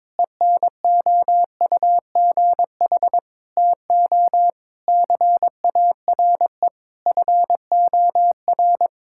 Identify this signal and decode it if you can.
Morse code, heard as a keyed tone, about 22 wpm, 700 Hz. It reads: ENOUGH TO CARE FOR
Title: Sentences from Top 1000 Words - Review Part 5 - 22wpm